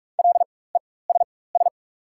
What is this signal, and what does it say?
Morse code, heard as a keyed tone, about 47 wpm, spaced as for 21 wpm, 690 Hz. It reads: LESS
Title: 4x Character Spacing - Top 500 Words - 45wpm